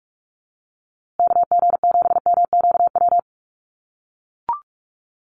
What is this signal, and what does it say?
Morse code, heard as a keyed tone, about 44 wpm, 700 Hz. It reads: KG7MQW
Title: USA Call Signs Repeated 2x 45wpm